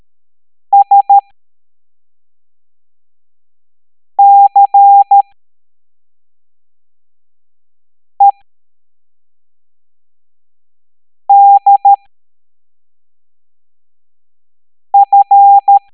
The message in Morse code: SCEDF